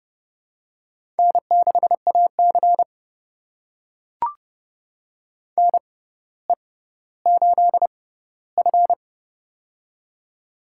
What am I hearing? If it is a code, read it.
Morse code, heard as a keyed tone, about 30 wpm, spaced as for 15 wpm, 700 Hz. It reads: N6AC NE8F
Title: USA Call Signs - 4 Characters - 6x Character Spacing - 30wpm